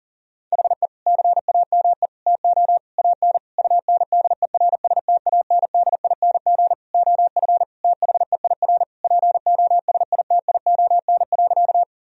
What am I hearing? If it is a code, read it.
Morse code, heard as a keyed tone, about 40 wpm, 700 Hz. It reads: HE CAME TO AN UNDERSTANDING OF THEIR POSITION.
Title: Sentences from Top 1000 Words - Review Part 3 - 40wpm